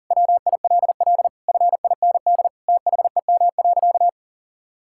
Morse code, heard as a keyed tone, about 37 wpm, 700 Hz. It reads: WILL FIND THEM.
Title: Sentences from Top 1000 Words - Review Part 6 - 40wpm